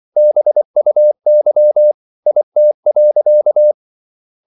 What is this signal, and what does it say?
Morse code, heard as a keyed tone, about 24 wpm, 600 Hz. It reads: BUY IT.